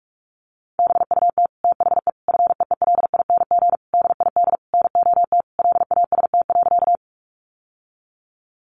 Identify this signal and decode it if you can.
Morse code, heard as a keyed tone, about 44 wpm, 700 Hz. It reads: BUT THE FEELING DID NOT LAST.
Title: Sentences from Top 1200 Words 45wpm